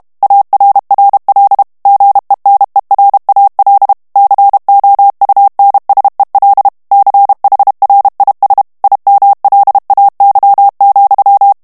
Morse code, heard as a keyed tone, about 32 wpm, 800 Hz. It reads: ARRL GENERAL COUNSEL CHRIS IMLAY,
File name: morse mp3 / ANGLAIS texte clair 3 vitesse 30 mots minute